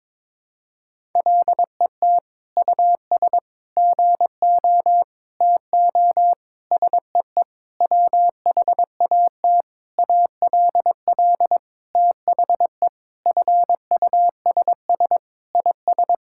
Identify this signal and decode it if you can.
Morse code, heard as a keyed tone, about 21 wpm, 700 Hz. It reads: LET US GO TO SEE WHAT ALL THE FUSS IS
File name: practice / 22wpm / Sentences with 8-Word Phrases - 22wpm